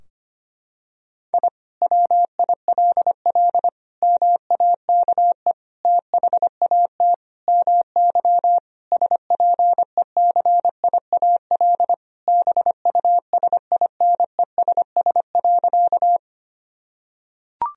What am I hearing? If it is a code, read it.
Morse code, heard as a keyed tone, about 25 wpm, 700 Hz. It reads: I WILL MAKE THAT MY SPECIAL BUSINESS.
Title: Sentences from Top 1000 Words - Review Part 3 - 25wpm